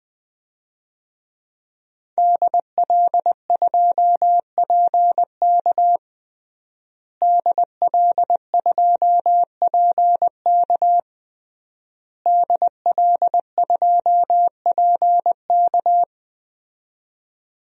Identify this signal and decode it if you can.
Morse code, heard as a keyed tone, about 21 wpm, 700 Hz. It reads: DL2PK DL2PK DL2PK
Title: DXCC-100 Call Signs - 5 Characters - Repeated 3x - 20wpm